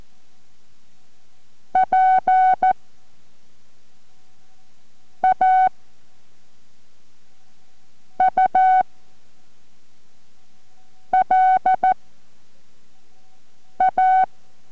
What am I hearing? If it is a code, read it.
Morse code, heard as a keyed tone, about 14 wpm, spaced as for 3 wpm, 750 Hz. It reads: PAUL A